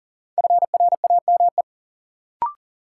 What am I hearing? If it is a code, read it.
Morse code, heard as a keyed tone, about 40 wpm, 700 Hz. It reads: FRAME
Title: Sentences from Top 1400 Words - New Word Drill Alt - 40wpm